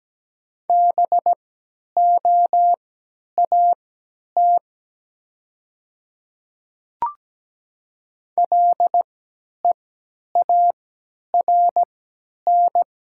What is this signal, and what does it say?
Morse code, heard as a keyed tone, about 17 wpm, spaced as for 10 wpm, 700 Hz. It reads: BOAT LEARN
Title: 3x Character Spacing - Top 500 Words - 17wpm